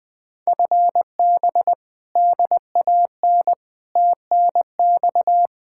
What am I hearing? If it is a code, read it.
Morse code, heard as a keyed tone, about 20 wpm, 700 Hz. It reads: FB DAN TNX
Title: Speed-Racing Common QSO Phrases Encore - 17wpm